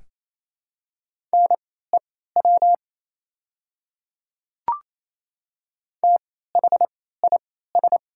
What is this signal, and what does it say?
Morse code, heard as a keyed tone, about 28 wpm, spaced as for 16 wpm, 700 Hz. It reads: NEW THIS
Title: 3x Character Spacing - Top 200 Words - 28wpm